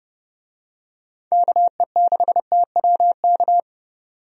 Morse code, heard as a keyed tone, about 30 wpm, 700 Hz. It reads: KE6TWK
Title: USA Call Signs - 6 Characters - 30wpm